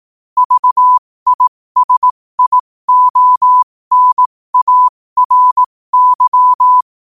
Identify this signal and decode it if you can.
Morse code, heard as a keyed tone, about 18 wpm, spaced as for 15 wpm, 1000 Hz. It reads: VISIONARY
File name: qotdfiles / 15wpm